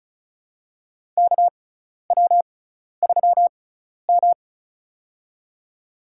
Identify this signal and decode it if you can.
Morse code, heard as a keyed tone, about 34 wpm, spaced as for 12 wpm, 700 Hz. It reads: KW3M
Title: USA Call Signs - 4 Characters - 6x Character Spacing - 35wpm